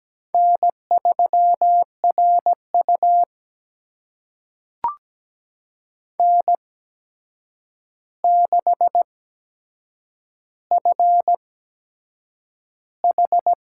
Morse code, heard as a keyed tone, about 17 wpm, spaced as for 7 wpm, 700 Hz. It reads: N3RU N6FH
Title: USA Call Signs - 4 Characters - 8x Character Spacing - 17wpm